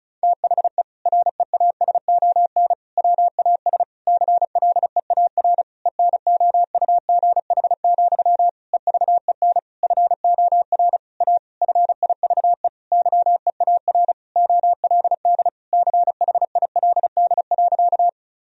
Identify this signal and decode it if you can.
Morse code, heard as a keyed tone, about 34 wpm, 700 Hz. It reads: THE REASON WAS CLEAR ENOUGH, EVEN FOR A FIVE YEAR OLD CHILD.
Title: Sentences from Top 1000 Words - Review Part 2 - 35wpm